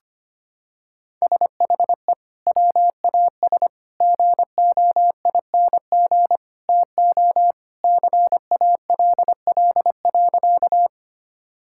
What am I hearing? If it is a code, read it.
Morse code, heard as a keyed tone, about 25 wpm, 700 Hz. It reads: SHE WAS GOING TO CALL.
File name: practice / 25wpm / Sentences with 3-Word Phrases - 25wpm